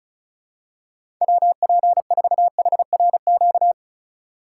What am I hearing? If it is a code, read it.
Morse code, heard as a keyed tone, about 36 wpm, 700 Hz. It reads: WP4HRQ
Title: USA Call Signs - 6 Characters - 2x Character Spacing - 35wpm